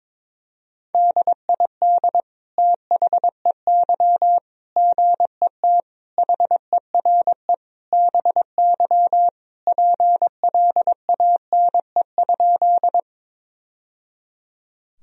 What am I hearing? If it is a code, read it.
Morse code, heard as a keyed tone, about 22 wpm, 700 Hz. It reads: DID THEY GET HERE BY PLANE?
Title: Sentences from Top 1000 Words - Review Part 3 - 22wpm